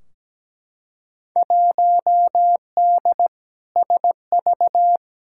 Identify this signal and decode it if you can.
Morse code, heard as a keyed tone, about 17 wpm, 700 Hz. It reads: 1D SV
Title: Field Day Exchanges - 17wpm